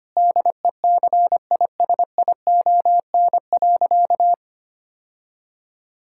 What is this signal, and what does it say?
Morse code, heard as a keyed tone, about 25 wpm, 700 Hz. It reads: DECISION.
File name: practice / 25wpm / Difficult Sentences with Hints - 25wpm